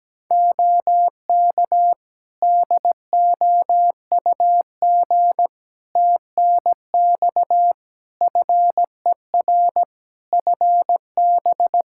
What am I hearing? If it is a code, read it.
Morse code, heard as a keyed tone, about 17 wpm, 700 Hz. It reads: OK DOUG TNX FER FB
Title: Practice QSO - Exchange 4 - Ending - 17wpm